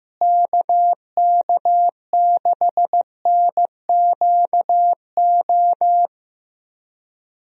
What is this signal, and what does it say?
Morse code, heard as a keyed tone, about 15 wpm, 700 Hz. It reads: KK6NQO
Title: USA Call Signs - 6 Characters - Repeated 2x - 15wpm